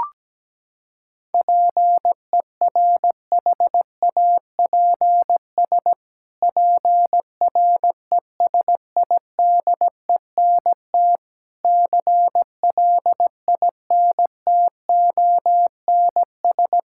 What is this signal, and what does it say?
Morse code, heard as a keyed tone, about 17 wpm, 700 Hz. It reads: PERHAPS PRESIDENT CLINTONS
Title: Pangram Challenge v2 - 17wpm